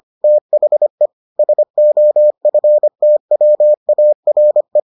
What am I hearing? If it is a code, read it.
Morse code, heard as a keyed tone, about 25 wpm, 600 Hz. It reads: THE SOFTWARE